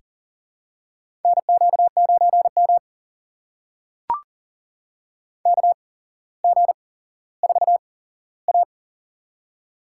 Morse code, heard as a keyed tone, about 38 wpm, spaced as for 17 wpm, 700 Hz. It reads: NQ9M KG4A
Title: USA Call Signs - 4 Characters - 8x Character Spacing - 40wpm